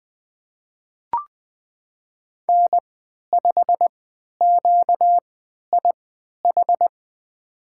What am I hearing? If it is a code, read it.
Morse code, heard as a keyed tone, about 20 wpm, spaced as for 11 wpm, 700 Hz. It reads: N5QIH
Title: USA Call Signs - 5 Characters - 3x Character Spacing - 20wpm